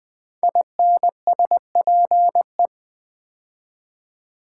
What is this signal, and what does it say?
Morse code, heard as a keyed tone, about 20 wpm, 700 Hz. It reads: INSPE
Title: Build A Word - 5 Letters At A Time - 10-Letter Words - 20wpm